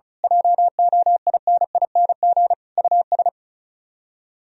Morse code, heard as a keyed tone, about 32 wpm, 700 Hz. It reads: JOINING US
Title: Speed-Racing - Sets of 5 Words 35wpm